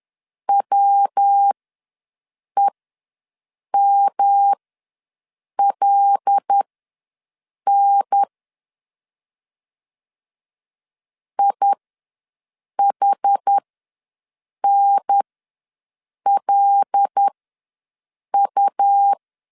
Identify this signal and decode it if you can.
Morse code, heard as a keyed tone, about 10 wpm, spaced as for 6 wpm, 800 Hz. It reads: WEMLN IHNLU
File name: LEZIONI / 5 LBVC / 5 L ws 10wpm 3636